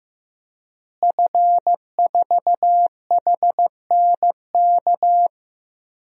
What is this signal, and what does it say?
Morse code, heard as a keyed tone, about 15 wpm, 700 Hz. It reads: F4HNK
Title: DXCC-100 Call Signs - 5 Characters - Repeated 2x - 15wpm